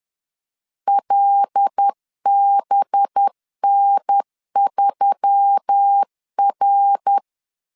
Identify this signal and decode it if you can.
Morse code, heard as a keyed tone, about 10 wpm, 800 Hz. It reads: LBN3R
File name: LEZIONI / 6 PKX0 / 6 PKX 10wpm 2426